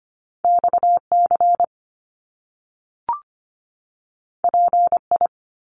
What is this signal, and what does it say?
Morse code, heard as a keyed tone, about 25 wpm, 700 Hz. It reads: XC PI
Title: Top Two Letter Combinations in Words 25wpm